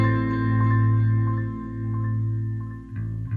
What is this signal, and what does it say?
There is no code.